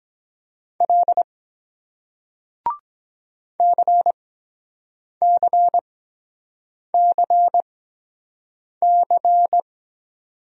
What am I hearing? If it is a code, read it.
Morse code, heard as a keyed tone, about 23 wpm, spaced as for 7 wpm, 700 Hz. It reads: L CCCC